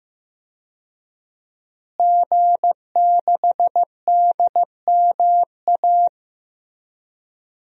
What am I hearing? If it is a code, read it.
Morse code, heard as a keyed tone, about 15 wpm, 700 Hz. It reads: G6DMA
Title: DXCC-100 Call Signs - 5 Characters - Repeated 3x - 15wpm